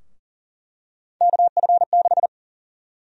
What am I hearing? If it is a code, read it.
Morse code, heard as a keyed tone, about 40 wpm, 700 Hz. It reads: KF6
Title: USA Call Sign Prefixes - Repeated 2x - 40wpm